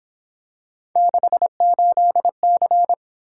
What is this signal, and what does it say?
Morse code, heard as a keyed tone, about 26 wpm, 700 Hz. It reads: 68C
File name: Speed-Racing - Three Letter-Number Combinations - 22wpm